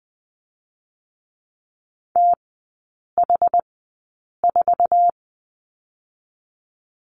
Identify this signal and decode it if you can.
Morse code, heard as a keyed tone, about 20 wpm, spaced as for 8 wpm, 700 Hz. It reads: TH4
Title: 060 Lesson 11e - Letter h - Three Characters 20wpm